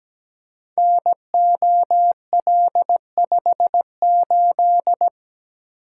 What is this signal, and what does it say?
Morse code, heard as a keyed tone, about 17 wpm, 700 Hz. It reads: NOL58
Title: Code Groups - 5 Letter:Numbers - 17wpm